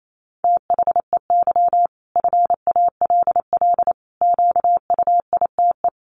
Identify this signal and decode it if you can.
Morse code, heard as a keyed tone, about 28 wpm, 700 Hz. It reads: THEY FALL QUITE
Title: Sentences from Top 1000 Words 28wpm